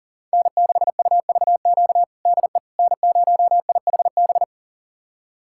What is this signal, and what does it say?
Morse code, heard as a keyed tone, about 37 wpm, 700 Hz. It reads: N6UVQ DE N0IHB
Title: USA Callsign DE Callsign - 5 Characters - 40wpm